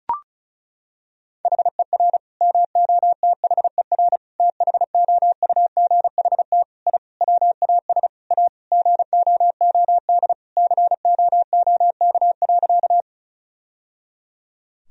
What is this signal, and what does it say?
Morse code, heard as a keyed tone, about 36 wpm, 700 Hz. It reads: HER MOTHER THOUGHT I WAS A GOOD COOK.
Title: Sentences from Top 1000 Words - Review Part 4 - 35wpm